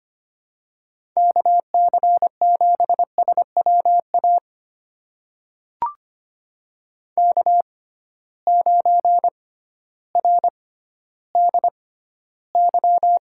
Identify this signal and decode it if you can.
Morse code, heard as a keyed tone, about 25 wpm, spaced as for 13 wpm, 700 Hz. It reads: KC7SWA K9RDY
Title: USA Call Signs - 6x Character Spacing - 25wpm